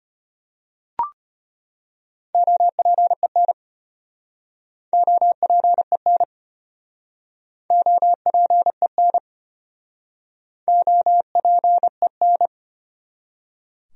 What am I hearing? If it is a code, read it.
Morse code, heard as a keyed tone, about 36 wpm, 700 Hz. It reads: OPEN OPEN OPEN OPEN